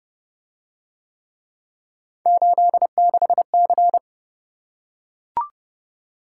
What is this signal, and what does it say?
Morse code, heard as a keyed tone, about 30 wpm, 700 Hz. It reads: 86C